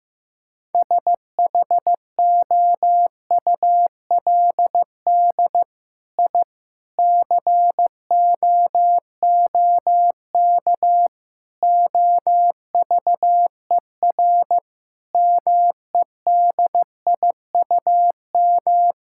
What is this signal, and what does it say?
Morse code, heard as a keyed tone, about 16 wpm, 700 Hz. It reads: SHOULD I COOK OVER MEDIUM
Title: Sentences from Top 1000 Words - Review Part 6 - 15wpm